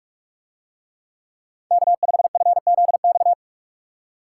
Morse code, heard as a keyed tone, about 49 wpm, 700 Hz. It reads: K5FZX